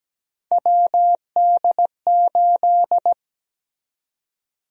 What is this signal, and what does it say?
Morse code, heard as a keyed tone, about 17 wpm, 700 Hz. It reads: WD8
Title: USA Call Sign Prefixes - 17wpm